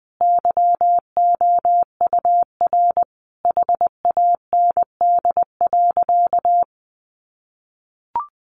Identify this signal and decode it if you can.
Morse code, heard as a keyed tone, about 19 wpm, 700 Hz. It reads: YOUR HAND.
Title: Sentences from Top 400 Words 20wpm